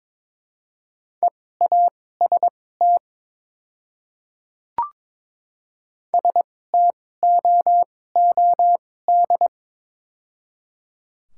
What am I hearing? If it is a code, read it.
Morse code, heard as a keyed tone, about 22 wpm, spaced as for 16 wpm, 700 Hz. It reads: EAST STOOD
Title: 2x Character Spacing - Top 500 Words - 22wpm